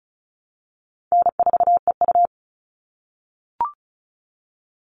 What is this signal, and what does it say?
Morse code, heard as a keyed tone, about 34 wpm, 700 Hz. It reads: N4EU
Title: All Cut Numbers - 4-Digit 35wpm